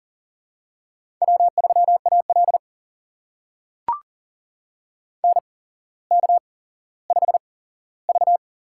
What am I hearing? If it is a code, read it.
Morse code, heard as a keyed tone, about 40 wpm, spaced as for 17 wpm, 700 Hz. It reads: W3AL NK5V
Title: USA Call Signs - 4 Characters - 8x Character Spacing - 40wpm